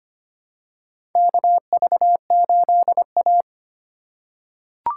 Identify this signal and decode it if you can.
Morse code, heard as a keyed tone, about 25 wpm, 700 Hz. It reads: KV8A
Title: USA Call Signs - 4 Characters - 3x Character Spacing - 25wpm